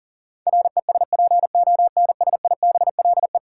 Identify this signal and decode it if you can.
Morse code, heard as a keyed tone, about 40 wpm, 700 Hz. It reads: RESPONSIBLE
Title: Sentences from Top 1600 Words - New Word Drill Alt - 40wpm